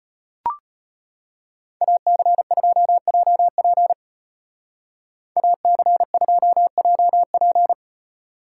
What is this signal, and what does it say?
Morse code, heard as a keyed tone, about 32 wpm, 700 Hz. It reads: AC2JP AC2JP